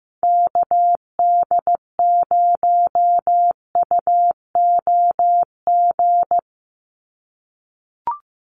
Morse code, heard as a keyed tone, about 15 wpm, 700 Hz. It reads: KD0UOG
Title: USA Call Signs 15wpm